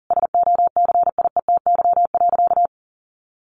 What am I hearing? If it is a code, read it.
Morse code, heard as a keyed tone, about 40 wpm, 700 Hz. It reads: SOCIETY.
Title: Sentences from Top 1500 Words 40wpm